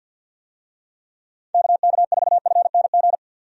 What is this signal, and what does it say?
Morse code, heard as a keyed tone, about 47 wpm, 700 Hz. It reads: KK4FNG
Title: USA Call Signs - 6 Characters - 50wpm